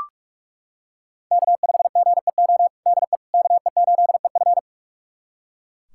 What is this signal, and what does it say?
Morse code, heard as a keyed tone, about 41 wpm, 700 Hz. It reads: K5GEO DE KE8EF
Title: USA Callsign DE Callsign - 5 Characters - 45wpm